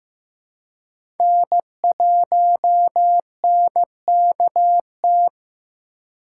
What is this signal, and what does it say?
Morse code, heard as a keyed tone, about 15 wpm, 700 Hz. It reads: N1NKT
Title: USA Call Signs - 5 Characters - 2x Character Spacing - 15wpm